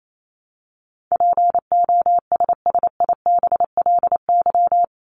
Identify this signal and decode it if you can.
Morse code, heard as a keyed tone, about 28 wpm, 700 Hz. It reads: POSSIBLY
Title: LY Suffix - Top 100 Words with Base Words 28wpm